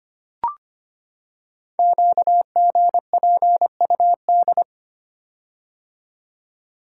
Morse code, heard as a keyed tone, about 25 wpm, 700 Hz. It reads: QGPUD
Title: Code Groups - 5 Letters - 25wpm